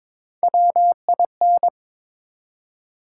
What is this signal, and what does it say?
Morse code, heard as a keyed tone, about 22 wpm, 700 Hz. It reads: WIN
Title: Three Letter Words - 22wpm